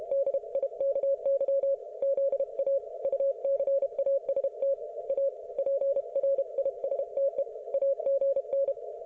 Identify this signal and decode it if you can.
Morse code, heard as a keyed tone, about 32 wpm, 550 Hz. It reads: DIKY ZA UCAST A PRISTE AGN